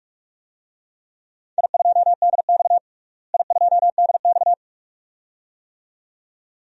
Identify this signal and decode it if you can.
Morse code, heard as a keyed tone, about 43 wpm, 700 Hz. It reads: I2DX I2DX
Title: DXCC-100 Call Signs - 4 Characters - Repeated 2x - 45wpm